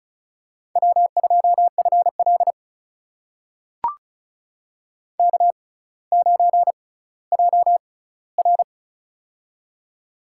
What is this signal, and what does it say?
Morse code, heard as a keyed tone, about 36 wpm, spaced as for 18 wpm, 700 Hz. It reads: W2FL K9JR
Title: USA Call Signs - 4 Characters - 6x Character Spacing - 35wpm